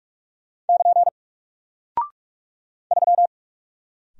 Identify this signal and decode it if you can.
Morse code, heard as a keyed tone, about 44 wpm, 700 Hz. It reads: <KN> 3
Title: Single Character (Letter:Number:Punctuation:Prosign) Alt 45wpm